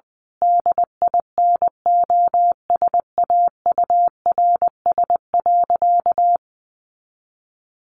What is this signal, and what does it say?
Morse code, heard as a keyed tone, about 20 wpm, 700 Hz. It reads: DINOSAURS.
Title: A Sound Of Thunder 20wpm